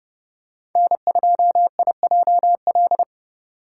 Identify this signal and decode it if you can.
Morse code, heard as a keyed tone, about 32 wpm, 700 Hz. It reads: N2IJL
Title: USA Call Signs - 5 Characters - 2x Character Spacing - 30wpm